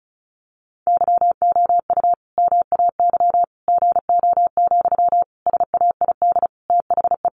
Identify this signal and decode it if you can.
Morse code, heard as a keyed tone, about 34 wpm, 700 Hz. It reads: YOU MAY GO, SAID THE
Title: Sentences from Top 400 Words 35wpm